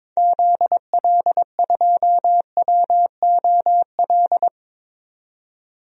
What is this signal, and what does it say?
Morse code, heard as a keyed tone, about 22 wpm, 700 Hz. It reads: ZL2WOL
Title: DXCC-100 Call Signs - 6 Characters - 22wpm